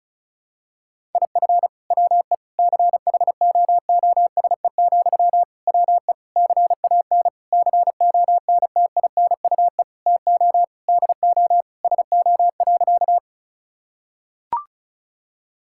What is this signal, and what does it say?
Morse code, heard as a keyed tone, about 34 wpm, 700 Hz. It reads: IF WE CHOOSE, WE CAN CONTINUE TO DO SO.
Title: Sentences from Top 1000 Words - Review Part 6 - 35wpm